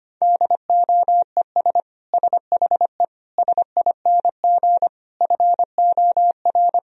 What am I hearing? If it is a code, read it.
Morse code, heard as a keyed tone, about 25 wpm, 700 Hz. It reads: DOES SHE SING FOR
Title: Sentences from Top 1000 Words - Review Part 2 - 25wpm